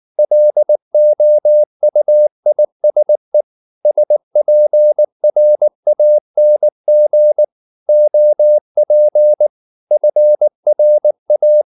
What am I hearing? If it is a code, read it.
Morse code, heard as a keyed tone, about 19 wpm, 600 Hz. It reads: LOUISE SPRANG OP FRA
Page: Ferie 19 wpm